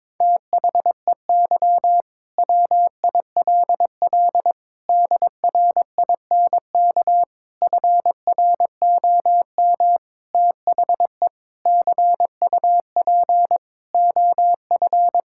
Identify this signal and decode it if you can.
Morse code, heard as a keyed tone, about 22 wpm, 700 Hz. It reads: THEY WILL DRINK FROM THE CUP OF
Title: Sentences from Top 1600 Words 22wpm